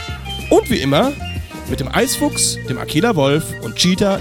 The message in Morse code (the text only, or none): none